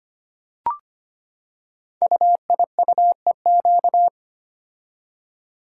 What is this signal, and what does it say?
Morse code, heard as a keyed tone, about 25 wpm, 700 Hz. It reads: UIUEQ
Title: Code Groups - 5 Letters - 25wpm